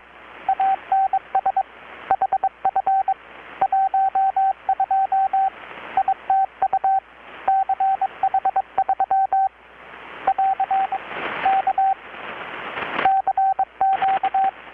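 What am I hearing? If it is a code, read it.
Morse code, heard as a keyed tone, about 23 wpm, 770 Hz, with radio noise. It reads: ANS HF 12 ITU CH3 <AR> K CQ
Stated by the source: Morse from KPH's Night of Nights XX maritime radiotelegraphy event, received in Chiapas, Mexico about 3,800 km away on a Kenwood R-600 with a grounded 10 m long-wire antenna.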